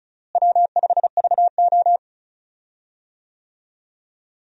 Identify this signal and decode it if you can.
Morse code, heard as a keyed tone, about 36 wpm, 700 Hz. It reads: W5VO
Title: USA Call Signs - 4 Characters - Repeated 2x - 35wpm